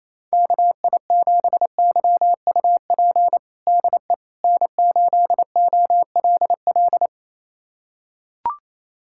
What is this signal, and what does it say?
Morse code, heard as a keyed tone, about 28 wpm, 700 Hz. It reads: KI7YUP DE N8OLL
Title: USA Callsign DE Callsign 28wpm